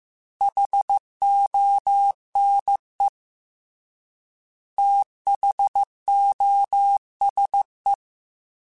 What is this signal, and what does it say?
Morse code, heard as a keyed tone, about 15 wpm, 800 Hz. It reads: HONE THOSE
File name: T then H Runs / T then H Runs.mp3s / T then H Runs @13